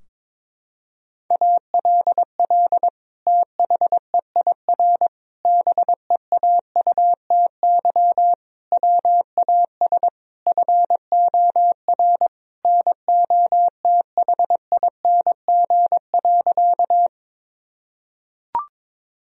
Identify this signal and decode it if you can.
Morse code, heard as a keyed tone, about 22 wpm, 700 Hz. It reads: ALL THEIR BEAUTY WAS FOR NOTHING.
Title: Sentences from Top 1000 Words - Review Part 2 - 22wpm